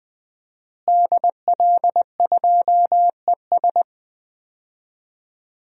Morse code, heard as a keyed tone, about 20 wpm, 700 Hz. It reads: DL2ES
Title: DXCC-100 Call Signs - 5 Characters - Repeated 2x - 20wpm